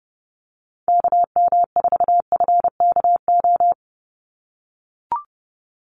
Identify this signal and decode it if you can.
Morse code, heard as a keyed tone, about 30 wpm, 700 Hz. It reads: KM4FKO
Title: USA Call Signs 30wpm